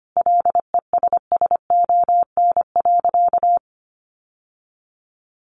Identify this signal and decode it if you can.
Morse code, heard as a keyed tone, about 25 wpm, 700 Hz. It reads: LESSON.
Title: Sentences from Top 1200 Words 25wpm